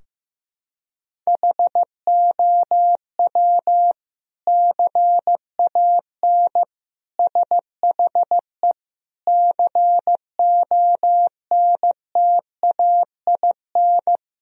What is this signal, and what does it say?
Morse code, heard as a keyed tone, about 16 wpm, 700 Hz. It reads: HOW CAN SHE CONTAIN
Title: Sentences from Top 1000 Words - Review Part 2 - 15wpm